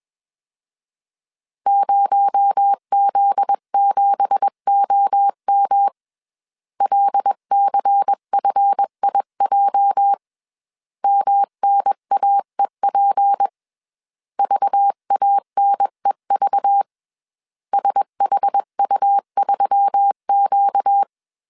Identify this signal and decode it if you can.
Morse code, heard as a keyed tone, about 21 wpm, 800 Hz. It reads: 0Z7OM LCFIJ MNAEP VANEV SHU3Q
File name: CW / DET / 1 9 1500C S B 20wpm 2313